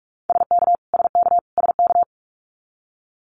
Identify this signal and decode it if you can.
Morse code, heard as a keyed tone, about 44 wpm, 700 Hz. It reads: SK SK SK
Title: Canadian Provinces and Territories - Repeated 3x - 45wpm